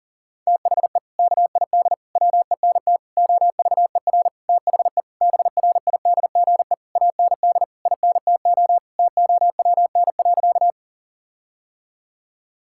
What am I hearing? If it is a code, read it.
Morse code, heard as a keyed tone, about 40 wpm, 700 Hz. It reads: THE KID WENT OVER THE BRIDGE AND INTO TOWN.
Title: Sentences from Top 1600 Words 40wpm